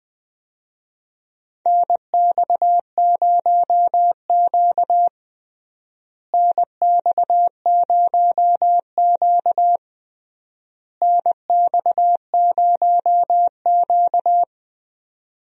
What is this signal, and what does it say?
Morse code, heard as a keyed tone, about 20 wpm, 700 Hz. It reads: NX0Q NX0Q NX0Q
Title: USA Call Signs - 4 Characters - Repeated 3x - 20wpm